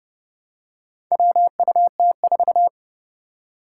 Morse code, heard as a keyed tone, about 29 wpm, 700 Hz. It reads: WUT4
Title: Four Challenging Letter-Number Combinations 30wpm